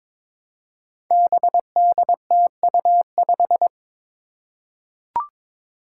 Morse code, heard as a keyed tone, about 22 wpm, 700 Hz. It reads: BDTU5
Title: Five Challenging Letter-Number Combinations 22wpm